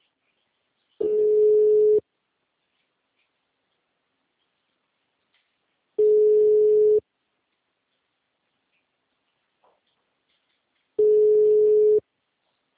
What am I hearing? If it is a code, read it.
Ringback tone (Europe, 425 Hz).